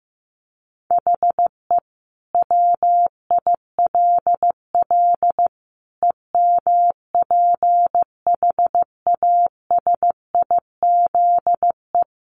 Morse code, heard as a keyed tone, about 15 wpm, 700 Hz. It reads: HE WILL EMPHASIZE
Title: Sentences from Top 1500 Words 15wpm